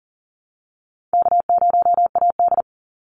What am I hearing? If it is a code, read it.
Morse code, heard as a keyed tone, about 40 wpm, 700 Hz. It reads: K0AD
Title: Easy Call Signs - 40wpm